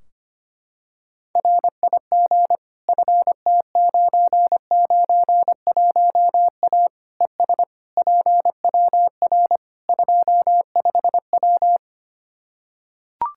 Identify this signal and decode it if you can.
Morse code, heard as a keyed tone, about 25 wpm, 700 Hz. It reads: RIG FT991A ES PWR 25W
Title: Practice QSO - Exchange 2 - Info - 25wpm